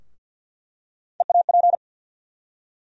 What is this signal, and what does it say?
Morse code, heard as a keyed tone, about 53 wpm, 700 Hz. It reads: EAP
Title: USA Call Sign Postfixes - 3 Letters 50wpm